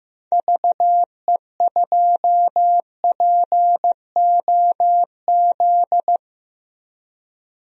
Morse code, heard as a keyed tone, about 15 wpm, 700 Hz. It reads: VE2POZ
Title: DXCC-100 Call Signs - 6 Characters - Repeated 2x - 15wpm